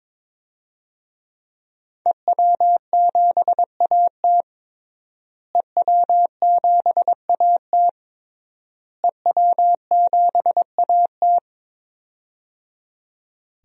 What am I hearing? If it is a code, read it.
Morse code, heard as a keyed tone, about 21 wpm, 700 Hz. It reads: EW7AT EW7AT EW7AT
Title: DXCC-100 Call Signs - 5 Characters - Repeated 3x - 22wpm